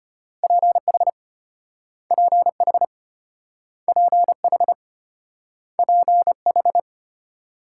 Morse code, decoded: PH PH PH PH

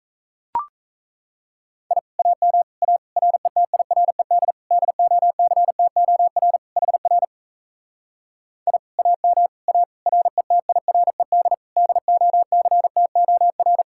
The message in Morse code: I AM A RETIRED DOCTOR HR I AM A RETIRED DOCTOR

